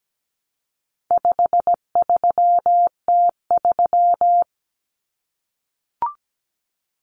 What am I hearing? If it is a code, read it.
Morse code, heard as a keyed tone, about 17 wpm, 700 Hz. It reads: 53T3
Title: Common Cut Numbers - 4-Digit 17wpm